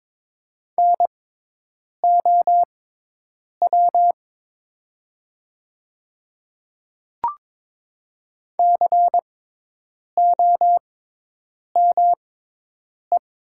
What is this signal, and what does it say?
Morse code, heard as a keyed tone, about 22 wpm, spaced as for 8 wpm, 700 Hz. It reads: NOW COME
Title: 6x Character Spacing - Top 100 Words - 22wpm